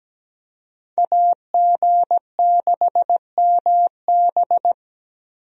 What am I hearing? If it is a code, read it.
Morse code, heard as a keyed tone, about 17 wpm, 700 Hz. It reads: AG6MB
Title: USA Call Signs - 5 Characters - Repeated 2x - 17wpm